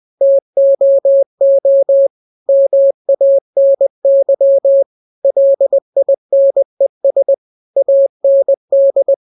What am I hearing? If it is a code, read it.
Morse code, heard as a keyed tone, about 20 wpm, 550 Hz. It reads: TOO MANY LINES AND
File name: jokes / joke 20 20 0000